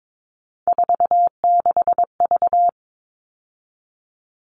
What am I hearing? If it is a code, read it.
Morse code, heard as a keyed tone, about 22 wpm, 700 Hz. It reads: 46V
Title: All Cut Numbers - 3-Digit 22wpm